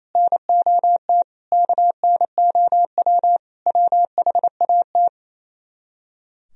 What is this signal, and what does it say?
Morse code, heard as a keyed tone, about 34 wpm, 700 Hz. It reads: NOT KNOW WHAT
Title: Speed-Racing - Sets of 5 Words 28wpm